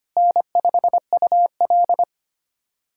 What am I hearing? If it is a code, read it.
Morse code, heard as a keyed tone, about 25 wpm, 700 Hz. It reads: N5UL
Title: USA Callsign DE Callsign - 4 Characters - 25wpm